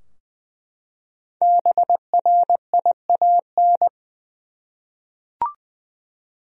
Morse code, heard as a keyed tone, about 20 wpm, 700 Hz. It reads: BRIAN
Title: USA Common Names - Max 5 Letters - 20wpm